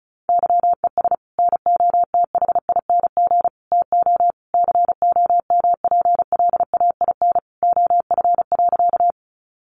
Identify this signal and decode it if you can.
Morse code, heard as a keyed tone, about 34 wpm, 700 Hz. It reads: YES NOTHING TO COMPLAIN OF.
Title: Sentences from Top 1300 Words 35wpm